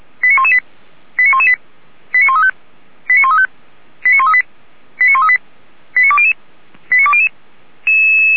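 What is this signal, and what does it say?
Five-tone selective call (ZVEI), heard as quick runs of five tones, tones of about 70 ms. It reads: 89118 89118 89135 89135 89138 89138 89200 89200
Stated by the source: Five-tone ZVEI selective call received by SDR.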